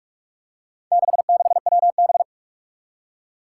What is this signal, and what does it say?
Morse code, heard as a keyed tone, about 44 wpm, 700 Hz. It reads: 66WB
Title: Four Challenging Letter-Number Combinations 45wpm